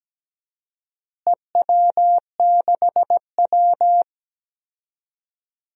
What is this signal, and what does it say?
Morse code, heard as a keyed tone, about 17 wpm, 700 Hz. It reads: EW6W
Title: DXCC-100 Call Signs - 4 Characters - Repeated 2x - 17wpm